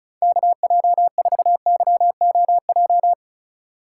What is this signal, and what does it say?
Morse code, heard as a keyed tone, about 36 wpm, 700 Hz. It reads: KJ4YOJ
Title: USA Call Signs - 8x Character Spacing - 35wpm